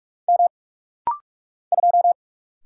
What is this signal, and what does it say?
Morse code, heard as a keyed tone, about 44 wpm, 700 Hz. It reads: M 2
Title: Single Letter-Number Alt - Rapid-Fire 45wpm